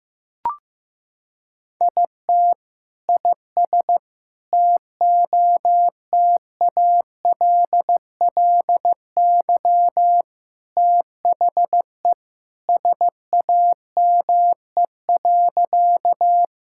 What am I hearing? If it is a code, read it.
Morse code, heard as a keyed tone, about 15 wpm, 700 Hz. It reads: IT IS TOTALLY THE SAME.
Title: Sentences from Top 1600 Words 15wpm